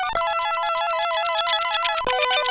There is no code.